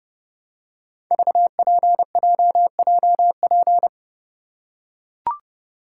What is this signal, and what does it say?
Morse code, heard as a keyed tone, about 28 wpm, 700 Hz. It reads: VPJJP
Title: Code Groups - 5 Letters - 30wpm